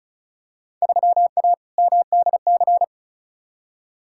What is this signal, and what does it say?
Morse code, heard as a keyed tone, about 36 wpm, 700 Hz. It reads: 3A MDC
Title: Field Day Exchanges - 35wpm